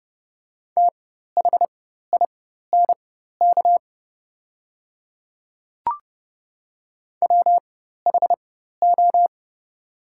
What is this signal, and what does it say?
Morse code, heard as a keyed tone, about 29 wpm, spaced as for 14 wpm, 700 Hz. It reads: THINK WHO
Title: 4x Character Spacing - Top 200 Words - 30wpm